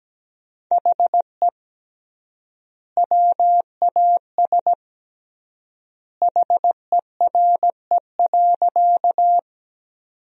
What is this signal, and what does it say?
Morse code, heard as a keyed tone, about 17 wpm, 700 Hz. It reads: HE WAS HERE.
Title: 3x Word-Spacing - Sentences from Top 200 Words - 17wpm